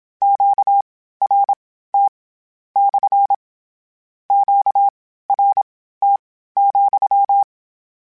Morse code, heard as a keyed tone, about 27 wpm, spaced as for 15 wpm, 800 Hz. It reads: QRT / QRT,
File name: Q Calls / Q Calls-mp3s / Q Calls SPSL2 @25